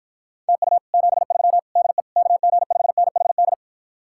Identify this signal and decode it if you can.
Morse code, heard as a keyed tone, about 52 wpm, 700 Hz. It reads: TU 73 DE KG5NHD